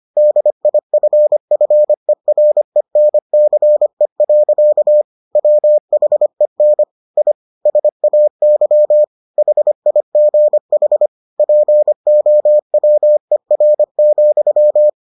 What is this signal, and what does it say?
Morse code, heard as a keyed tone, about 25 wpm, 600 Hz. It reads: DIFFERENCE. WHEN I SAY HIGH POWER,